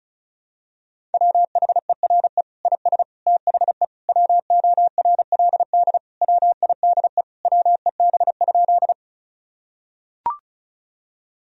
Morse code, heard as a keyed tone, about 34 wpm, 700 Hz. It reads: WHERE IS THE WORLD WIDE WEB?
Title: Sentences from Top 1000 Words - Review Part 4 - 35wpm